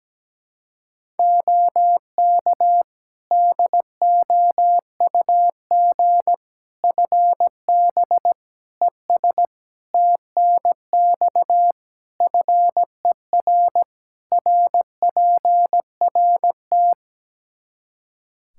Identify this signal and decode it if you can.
Morse code, heard as a keyed tone, about 17 wpm, 700 Hz. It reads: OK DOUG FB ES TNX FER RPRT
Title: Practice QSO - Exchange 2 - Info - 17wpm